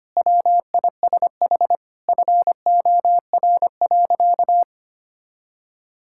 Morse code, heard as a keyed tone, about 25 wpm, 700 Hz. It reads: WISH FOR.
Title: Sentences from Top 1600 Words 25wpm